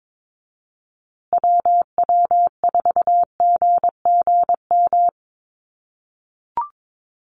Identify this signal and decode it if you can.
Morse code, heard as a keyed tone, about 22 wpm, 700 Hz. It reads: WW4GGM
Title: USA Call Signs Repeated 2x 22wpm